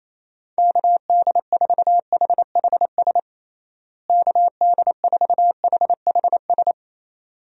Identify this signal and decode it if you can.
Morse code, heard as a keyed tone, about 28 wpm, 700 Hz. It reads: KD4HHS KD4HHS
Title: USA Call Signs - 6 Characters - Repeated 2x - 28wpm